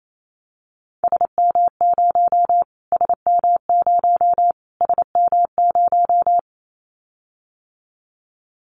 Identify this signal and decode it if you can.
Morse code, heard as a keyed tone, about 28 wpm, 700 Hz. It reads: SM0 SM0 SM0
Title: Callsign Prefixes for Top 25 DXCC - Repeated 3x - 28wpm